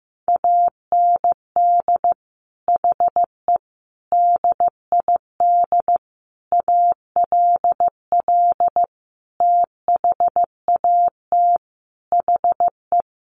Morse code, heard as a keyed tone, about 15 wpm, 700 Hz. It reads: AND HE DID ALL THAT HE
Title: Sentences from Top 1500 Words 15wpm